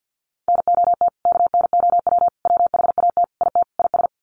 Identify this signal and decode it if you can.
Morse code, heard as a keyed tone, about 52 wpm, 700 Hz. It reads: NOT KNOW WHAT IT IS
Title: Sets of 5 Words Encore 50wpm